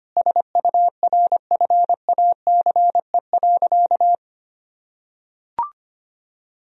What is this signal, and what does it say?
Morse code, heard as a keyed tone, about 25 wpm, 700 Hz. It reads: SURFACE.
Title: Sentences from Top 1000 Words - Review Part 3 - 25wpm